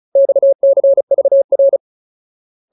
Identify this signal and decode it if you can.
Morse code, heard as a keyed tone, about 34 wpm, 550 Hz. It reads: XCVR
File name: Ultimate Word List / 35WPM Ultimate